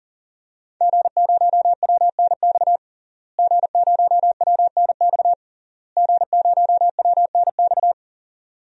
Morse code, heard as a keyed tone, about 40 wpm, 700 Hz. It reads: G0WNX G0WNX G0WNX
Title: DXCC-100 Call Signs - 5 Characters - Repeated 3x - 40wpm